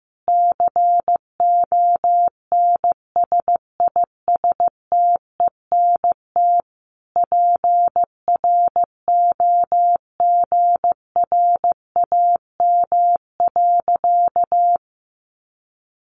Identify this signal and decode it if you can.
Morse code, heard as a keyed tone, about 15 wpm, 700 Hz. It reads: CONSISTENT PROGRAM.
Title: Sentences from Top 1500 Words 15wpm